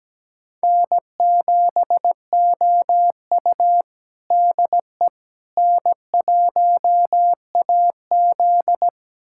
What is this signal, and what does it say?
Morse code, heard as a keyed tone, about 17 wpm, 700 Hz. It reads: N7OU DE N1AZ
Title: USA Callsign DE Callsign - 4 Characters - 17wpm